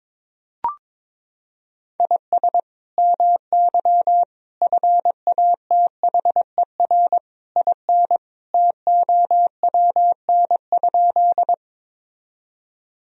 Morse code, heard as a keyed tone, about 22 wpm, 700 Hz. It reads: IS MY FATHER IN TOWN?
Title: Sentences from Top 1000 Words - Review Part 2 - 22wpm